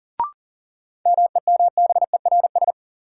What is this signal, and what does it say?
Morse code, heard as a keyed tone, about 40 wpm, 700 Hz. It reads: MEMBERS
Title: Sets of 3 Words - New Word Drill - 40wpm